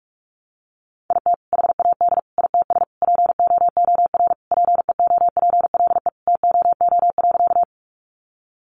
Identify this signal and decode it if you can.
Morse code, heard as a keyed tone, about 44 wpm, 700 Hz. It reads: IT HAD ITS POOR PEOPLE TOO.
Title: Sentences from Top 1000 Words 45wpm